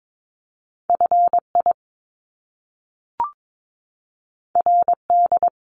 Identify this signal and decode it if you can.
Morse code, heard as a keyed tone, about 22 wpm, 700 Hz. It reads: FI RD
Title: Top Two Letter Combinations in Words 22wpm